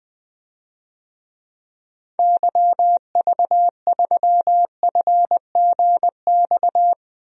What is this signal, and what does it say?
Morse code, heard as a keyed tone, about 20 wpm, 700 Hz. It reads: YV3FGX
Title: DXCC-100 Call Signs - 6 Characters - Repeated 2x - 20wpm